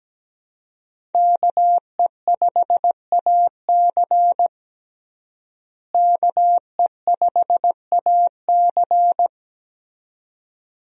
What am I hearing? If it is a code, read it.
Morse code, heard as a keyed tone, about 17 wpm, 700 Hz. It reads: KE5AC KE5AC
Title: USA Call Signs - 5 Characters - Repeated 2x - 17wpm